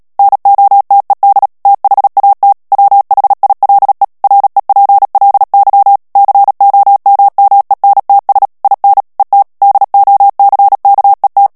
Morse code, heard as a keyed tone, about 37 wpm, 800 Hz. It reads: NOTED THAT WHILE REPLY COMMENTS IN ET DOCKET
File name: morse mp3 / ANGLAIS texte clair 3 vitesse 35 mots minute